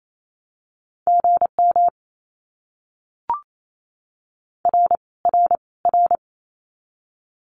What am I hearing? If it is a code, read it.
Morse code, heard as a keyed tone, about 28 wpm, 700 Hz. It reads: GM R R R
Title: Abbreviations Repeated 3x - 28wpm